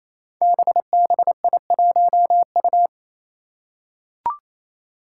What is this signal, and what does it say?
Morse code, heard as a keyed tone, about 27 wpm, 700 Hz. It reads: BBI1U
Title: Five Challenging Letter-Number Combinations 28wpm